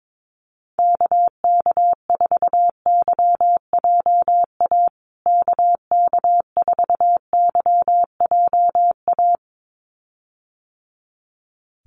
Morse code, heard as a keyed tone, about 22 wpm, 700 Hz. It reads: KK4YJA KK4YJA
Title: USA Call Signs Repeated 2x 22wpm